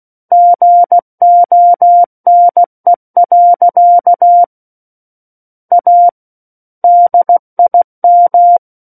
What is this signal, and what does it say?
Morse code, heard as a keyed tone, about 16 wpm, 700 Hz. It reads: GONE. A DIM
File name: Book One / 07-Chapter 7 - 16 WPM